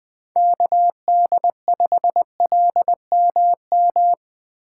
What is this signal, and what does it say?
Morse code, heard as a keyed tone, about 20 wpm, 700 Hz. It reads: KD5LMM
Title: Speed Racing - USA 2x3 Call Signs 17wpm